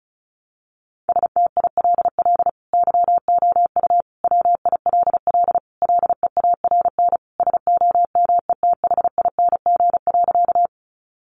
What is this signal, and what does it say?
Morse code, heard as a keyed tone, about 34 wpm, 700 Hz. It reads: STILL YOU WILL LEARN SOMETHING.
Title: Sentences from Top 600 Words 35wpm